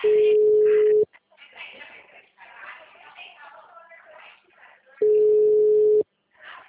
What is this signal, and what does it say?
Ringback tone (Europe, 425 Hz), with speech.
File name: out- -2009- - - .270